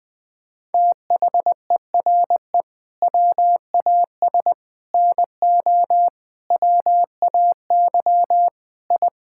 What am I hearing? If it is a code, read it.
Morse code, heard as a keyed tone, about 20 wpm, 700 Hz. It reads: THERE WAS NO WAY I
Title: Sentences with 7-Word Phrases - 20wpm